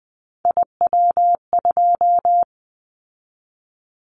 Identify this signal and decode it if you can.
Morse code, heard as a keyed tone, about 19 wpm, 700 Hz. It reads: IW2
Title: Callsign Prefixes for Top 25 DXCC - 20wpm